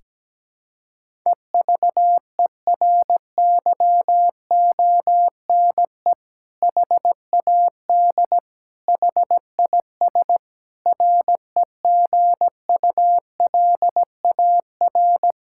Morse code, heard as a keyed tone, about 17 wpm, 700 Hz. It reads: EVERYONE HAD HIS REGULAR
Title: Sentences from Top 1000 Words - Review Part 6 - 17wpm